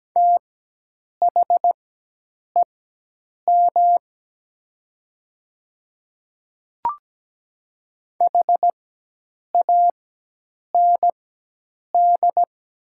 Morse code, heard as a keyed tone, about 17 wpm, spaced as for 8 wpm, 700 Hz. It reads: THEM HAND
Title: 4x Character Spacing - Top 500 Words - 17wpm